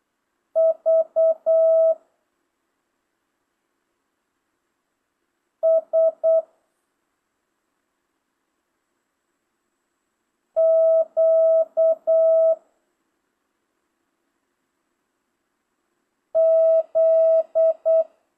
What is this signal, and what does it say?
Morse code, heard as a keyed tone, about 8 wpm, 650 Hz. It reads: V S Q Z
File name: jh8cbh / 2025cw